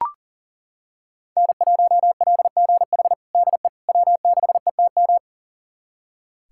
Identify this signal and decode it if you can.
Morse code, heard as a keyed tone, about 37 wpm, 700 Hz. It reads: N1LGH DE W6ETM
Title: USA Callsign DE Callsign - 5 Characters - 40wpm